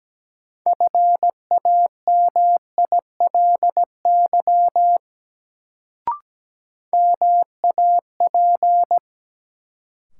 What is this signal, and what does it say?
Morse code, heard as a keyed tone, about 17 wpm, 700 Hz. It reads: FAMILY MAP
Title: Sentences from Top 400 Words - New Word Drill - 17wpm